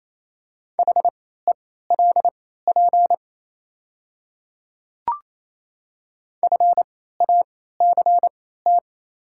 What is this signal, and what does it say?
Morse code, heard as a keyed tone, about 28 wpm, spaced as for 16 wpm, 700 Hz. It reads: HELP FACT
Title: 3x Character Spacing - Top 500 Words - 28wpm